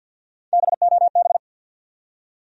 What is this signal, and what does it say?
Morse code, heard as a keyed tone, about 50 wpm, 700 Hz. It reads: BOB
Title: USA Common Names - Max 3 Letters - 50wpm